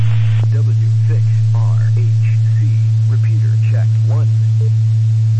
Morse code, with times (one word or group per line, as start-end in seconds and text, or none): none